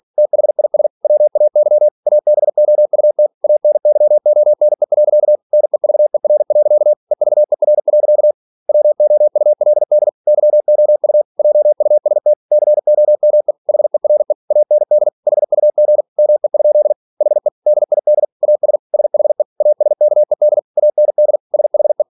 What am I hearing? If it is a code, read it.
Morse code, heard as a keyed tone, about 46 wpm, 600 Hz. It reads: THIS WAY ABOUT ANYONE. NEVER. EVER. WOULD YOU JUST COME HERE AND HUG ME? HE DID AS SHE ASKED AND SHE
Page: Never 47wpm